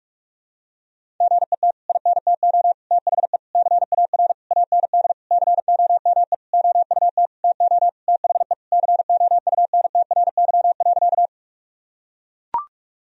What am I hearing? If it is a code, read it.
Morse code, heard as a keyed tone, about 41 wpm, 700 Hz. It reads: GET INTO THE CAR AND COME OUT TO THE COUNTRY.
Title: Sentences from Top 1000 Words - Review Part 1 - 45wpm